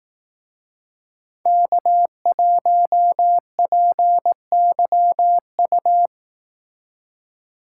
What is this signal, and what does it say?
Morse code, heard as a keyed tone, about 18 wpm, 700 Hz. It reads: K1PYU
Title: Speed Racing - USA 1x3 Call Signs 15wpm